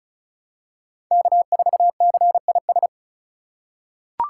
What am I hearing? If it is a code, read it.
Morse code, heard as a keyed tone, about 36 wpm, 690 Hz. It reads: K4CIS
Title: USA Call Signs - 5 Characters - 3x Character Spacing - 35wpm